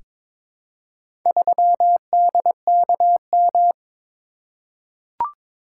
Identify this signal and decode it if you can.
Morse code, heard as a keyed tone, about 22 wpm, 700 Hz. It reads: 3DKM
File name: practice / 22wpm / Four Letter-Number Combinations - 22wpm